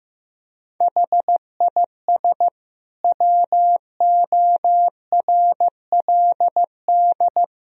Morse code, heard as a keyed tone, about 15 wpm, 700 Hz. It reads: HIS WORLD
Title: Sentences from Top 1000 Words - Review Part 1 - 15wpm